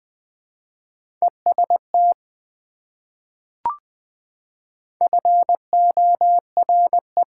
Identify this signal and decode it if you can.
Morse code, heard as a keyed tone, about 20 wpm, 700 Hz. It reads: EST FORE
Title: Prefixes and Suffixes 20wpm